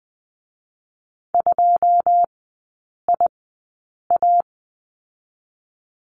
Morse code, heard as a keyed tone, about 19 wpm, spaced as for 8 wpm, 700 Hz. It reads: 2IA
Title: 218 Lesson 35e - Number 0 - Three Characters 20wpm